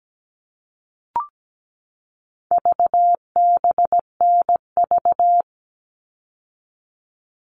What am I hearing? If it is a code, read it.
Morse code, heard as a keyed tone, about 17 wpm, 700 Hz. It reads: VBNV
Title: All Cut Numbers - 4-Digit 17wpm